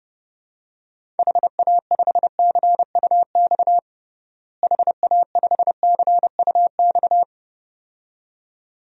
Morse code, heard as a keyed tone, about 30 wpm, 700 Hz. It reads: HA5CUX HA5CUX
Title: DXCC-100 Call Signs - 6 Characters - Repeated 2x - 30wpm